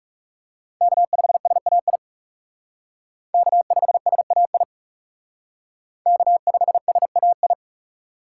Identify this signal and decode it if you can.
Morse code, heard as a keyed tone, about 37 wpm, 700 Hz. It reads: K5SAI K5SAI K5SAI
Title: Speed Racing - USA 1x3 Call Signs 30wpm